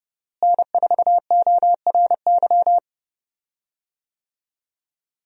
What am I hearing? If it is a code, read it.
Morse code, heard as a keyed tone, about 29 wpm, 700 Hz. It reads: N4ORY
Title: USA Call Signs - 5 Characters - Repeated 2x - 30wpm